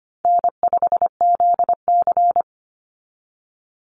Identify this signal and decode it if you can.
Morse code, heard as a keyed tone, about 25 wpm, 700 Hz. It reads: N5ZC
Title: NA SOTA Chaser Callsigns 25wpm